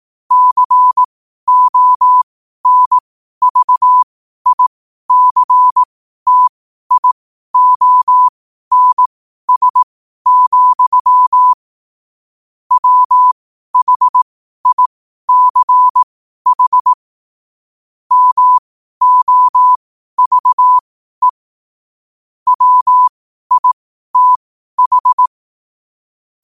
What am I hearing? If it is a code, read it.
Morse code, heard as a keyed tone, about 18 wpm, spaced as for 12 wpm, 1000 Hz. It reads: CONVICTIONS, WHICH MOVE WITH
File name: qotdfiles / 12wpm